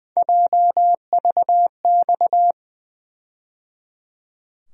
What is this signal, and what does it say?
Morse code, heard as a keyed tone, about 20 wpm, 700 Hz. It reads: JVX
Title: USA Call Sign Postfixes - 3 Letters 20wpm